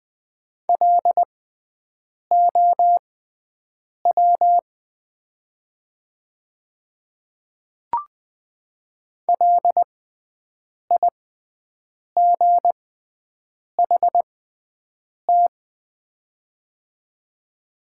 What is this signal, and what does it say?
Morse code, heard as a keyed tone, about 20 wpm, spaced as for 7 wpm, 700 Hz. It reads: LOW LIGHT
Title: 6x Character Spacing - Top 200 Words - 20wpm